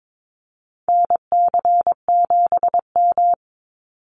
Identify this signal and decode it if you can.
Morse code, heard as a keyed tone, about 22 wpm, 700 Hz. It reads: NC7M
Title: Call Signs Repeated 3x - 22wpm